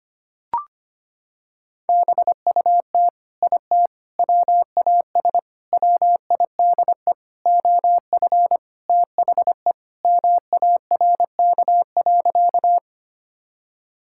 Morse code, heard as a keyed tone, about 25 wpm, 700 Hz. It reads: BUT IT WAS WIDE OF THE MARK.
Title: Sentences from Top 1000 Words - Review Part 6 - 25wpm